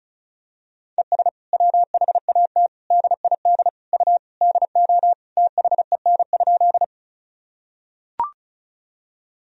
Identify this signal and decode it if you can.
Morse code, heard as a keyed tone, about 36 wpm, 700 Hz. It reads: ES WHAT DID U DO THEN?
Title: Sentences from Top 100 Words and Abbreviations 35wpm